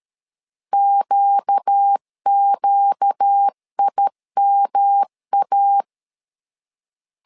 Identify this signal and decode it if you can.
Morse code, heard as a keyed tone, about 13 wpm, 800 Hz. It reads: QQIMA
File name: LEZIONI / 4 Q 12wpm 1833